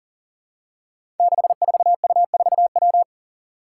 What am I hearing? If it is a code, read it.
Morse code, heard as a keyed tone, about 41 wpm, 700 Hz. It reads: 64U4W
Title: Five Challenging Letter-Number Combinations 40wpm